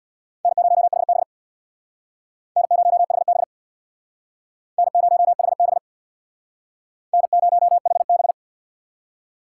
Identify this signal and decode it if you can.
Morse code, heard as a keyed tone, about 61 wpm, 700 Hz. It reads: N0HB N0HB N0HB N0HB
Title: Speed Racing - USA 1x2 Call Signs 50wpm